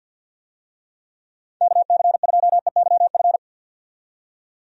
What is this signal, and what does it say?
Morse code, heard as a keyed tone, about 52 wpm, 700 Hz. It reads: KC2EYF
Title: USA Call Signs - 6 Characters - 50wpm